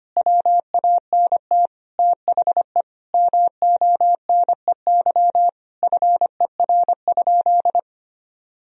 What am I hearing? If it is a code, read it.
Morse code, heard as a keyed tone, about 25 wpm, 700 Hz. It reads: WANT THE MONEY FER?
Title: Sentences from Top 400 Words and Abbreviations 25wpm